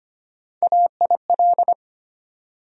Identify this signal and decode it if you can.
Morse code, heard as a keyed tone, about 25 wpm, 700 Hz. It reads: AIL
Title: USA Call Sign Postfixes - 3 Letters 25wpm